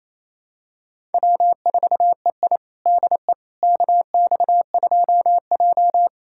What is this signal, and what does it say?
Morse code, heard as a keyed tone, about 27 wpm, 700 Hz. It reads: W4EI DE KX2J
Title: USA Callsign DE Callsign - 4 Characters - 28wpm